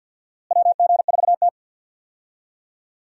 Morse code, heard as a keyed tone, about 50 wpm, 700 Hz. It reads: WG4T
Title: USA Call Signs - 4 Characters - Repeated 2x - 50wpm